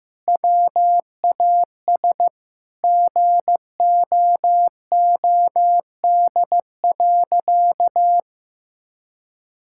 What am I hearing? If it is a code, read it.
Morse code, heard as a keyed tone, about 16 wpm, 700 Hz. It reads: WAS GOOD.
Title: Sentences from Top 1000 Words - Review Part 6 - 15wpm